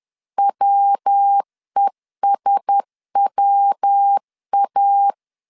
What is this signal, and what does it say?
Morse code, heard as a keyed tone, about 10 wpm, 800 Hz. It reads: WESWA
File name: CW / DET / 1 6 1500C S B 10wpm 4137